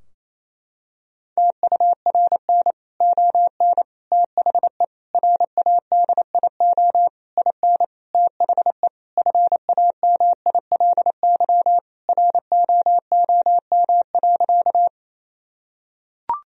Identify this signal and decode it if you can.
Morse code, heard as a keyed tone, about 28 wpm, 700 Hz. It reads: TURN ON THE RADIO IN THE FAMILY ROOM.
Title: Sentences from Top 1000 Words - Review Part 3 - 28wpm